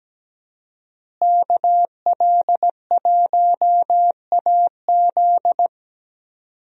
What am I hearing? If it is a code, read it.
Morse code, heard as a keyed tone, about 17 wpm, 700 Hz. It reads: KL1AZ
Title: USA Call Signs - 5 Characters - Repeated 3x - 17wpm